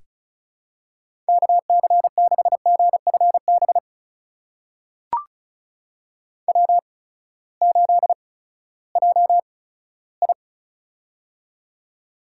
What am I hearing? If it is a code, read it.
Morse code, heard as a keyed tone, about 36 wpm, spaced as for 18 wpm, 700 Hz. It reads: KC6GFB W8JI
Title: USA Call Signs - 8x Character Spacing - 35wpm